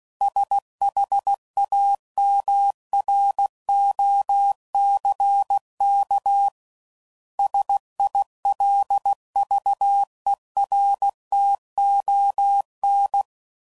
Morse code, heard as a keyed tone, about 16 wpm, 800 Hz. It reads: SHAMROCK SILVERTON
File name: CW Code Groups @15WPM